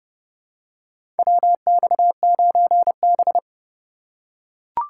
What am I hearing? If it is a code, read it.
Morse code, heard as a keyed tone, about 30 wpm, 700 Hz. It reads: WX9B
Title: USA Call Signs - 4 Characters - 4x Character Spacing - 30wpm